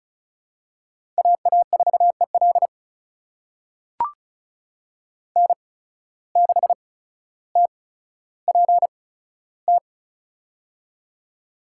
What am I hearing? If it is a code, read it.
Morse code, heard as a keyed tone, about 36 wpm, spaced as for 15 wpm, 700 Hz. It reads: AA4EL N6TPT
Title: USA Call Signs - 5 Characters - 8x Character Spacing - 35wpm